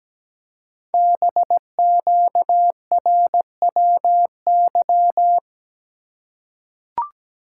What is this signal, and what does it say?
Morse code, heard as a keyed tone, about 17 wpm, 700 Hz. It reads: BQRWY
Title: Code Groups - 5 Letters - 17wpm